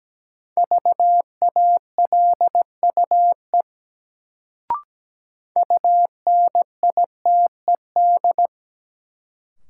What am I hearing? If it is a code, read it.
Morse code, heard as a keyed tone, about 17 wpm, 700 Hz. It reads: VALUE UNITED
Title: Sentences from Top 600 Words - New Word Drill - 17wpm